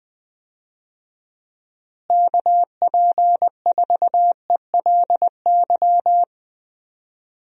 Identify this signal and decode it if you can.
Morse code, heard as a keyed tone, about 20 wpm, 700 Hz. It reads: KP4ELY
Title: USA Call Signs - 6 Characters - Repeated 3x - 20wpm